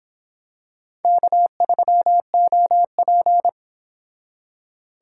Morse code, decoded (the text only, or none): K3OP